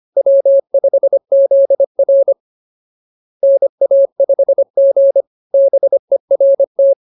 Morse code, heard as a keyed tone, about 25 wpm, 550 Hz. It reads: W5ZR NA5G BERT
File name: CWT-209-25